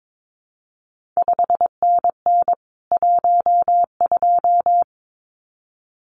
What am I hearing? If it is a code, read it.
Morse code, heard as a keyed tone, about 22 wpm, 700 Hz. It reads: 5NN 12
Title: CQ WW DX Contest Exchange Encore - 22wpm